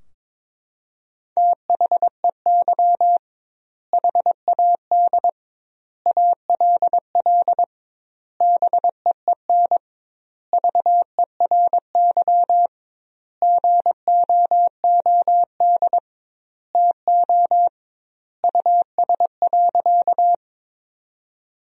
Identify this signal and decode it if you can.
Morse code, heard as a keyed tone, about 22 wpm, 700 Hz. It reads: THEY HAD ALL BEEN VERY GOOD TO US.
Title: 2x Word-Spacing - Sentences from Top 200 Words - 22wpm